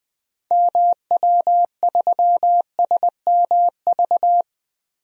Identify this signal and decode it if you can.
Morse code, heard as a keyed tone, about 20 wpm, 700 Hz. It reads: MW3SMV
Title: DXCC-100 Call Signs - 6 Characters - Repeated 2x - 20wpm